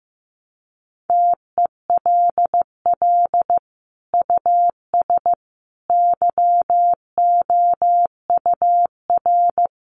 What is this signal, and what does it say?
Morse code, heard as a keyed tone, about 14 wpm, 700 Hz. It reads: TELL US YOUR
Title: Sentences from Top 1400 Words 15wpm